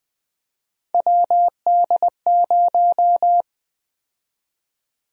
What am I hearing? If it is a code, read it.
Morse code, heard as a keyed tone, about 20 wpm, 700 Hz. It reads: WD0
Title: USA Call Sign Prefixes - Repeated 2x - 20wpm